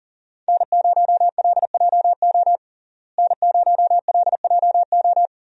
Morse code, heard as a keyed tone, about 37 wpm, 700 Hz. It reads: N0LJO N0LJO
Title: USA Call Signs - 5 Characters - Repeated 3x - 40wpm